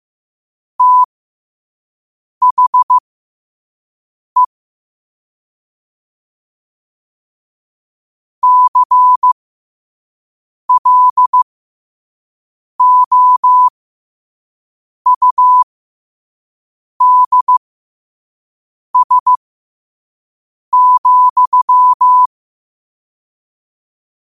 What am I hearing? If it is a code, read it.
Morse code, heard as a keyed tone, about 15 wpm, spaced as for 5 wpm, 1000 Hz. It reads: THE CLOUDS,
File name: qotdfiles / 5wpm